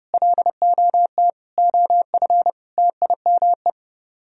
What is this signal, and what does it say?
Morse code, heard as a keyed tone, about 37 wpm, 700 Hz. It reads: LOT OF TIME
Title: Speed-Racing - Sets of 4 Words 30wpm